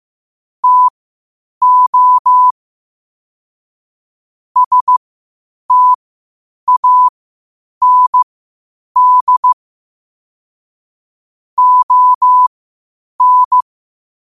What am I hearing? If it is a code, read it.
Morse code, heard as a keyed tone, about 15 wpm, spaced as for 8 wpm, 1000 Hz. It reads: TO STAND ON
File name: 8wpm